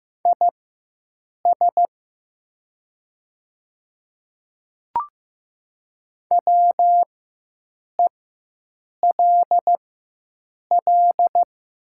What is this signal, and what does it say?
Morse code, heard as a keyed tone, about 15 wpm, spaced as for 7 wpm, 700 Hz. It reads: IS WELL
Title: 4x Character Spacing - Top 200 Words - 15wpm